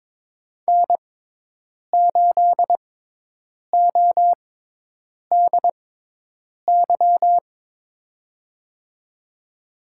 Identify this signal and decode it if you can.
Morse code, heard as a keyed tone, about 21 wpm, spaced as for 8 wpm, 700 Hz. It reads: N8ODY
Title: USA Call Signs - 5 Characters - 6x Character Spacing - 22wpm